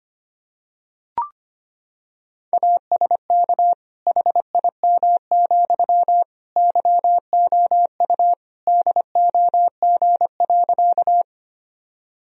Morse code, heard as a keyed tone, about 25 wpm, 700 Hz. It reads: ASK HIM, YOU DOG.
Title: Sentences from Top 1000 Words - Review Part 2 - 25wpm